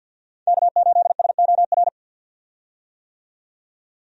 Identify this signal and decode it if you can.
Morse code, heard as a keyed tone, about 46 wpm, 700 Hz. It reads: K8SOR
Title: USA Call Signs - 5 Characters - 50wpm